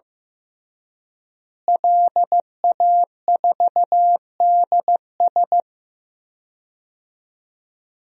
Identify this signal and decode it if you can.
Morse code, heard as a keyed tone, about 16 wpm, 700 Hz. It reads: LA4DS
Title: DXCC-100 Call Signs - 5 Characters - Repeated 3x - 15wpm